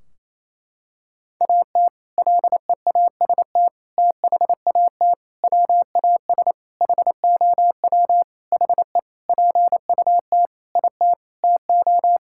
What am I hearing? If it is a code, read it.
Morse code, heard as a keyed tone, about 28 wpm, 700 Hz. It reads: AT LEAST THAT WAS HOW HE PUT IT TO
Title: Sentences from Top 1600 Words 28wpm